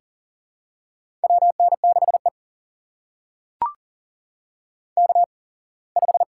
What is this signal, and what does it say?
Morse code, heard as a keyed tone, about 40 wpm, spaced as for 24 wpm, 700 Hz. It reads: WN6E K5
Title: USA Call Signs - 4 Characters - 8x Character Spacing - 40wpm